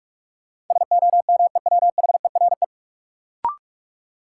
Morse code, heard as a keyed tone, about 41 wpm, 700 Hz. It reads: SOMEWHERE
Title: Sentences from Top 1400 Words - New Word Drill Alt - 45wpm